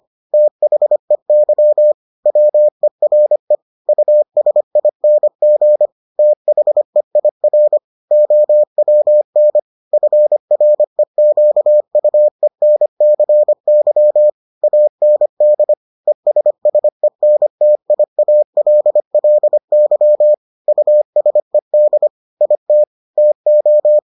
Morse code, heard as a keyed tone, about 25 wpm, 600 Hz. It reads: THEY WERE USING THEIR OWN FREQUENCY AND ESSENTIALLY USED IT TO